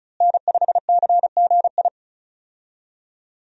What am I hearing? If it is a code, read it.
Morse code, heard as a keyed tone, about 36 wpm, 700 Hz. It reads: N5CGI
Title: USA Call Signs - 5 Characters - Repeated 2x - 35wpm